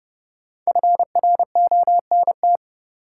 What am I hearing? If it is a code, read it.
Morse code, heard as a keyed tone, about 30 wpm, 700 Hz. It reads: FRONT
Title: Sentences from Top 500 Words - New Word Drill - 30wpm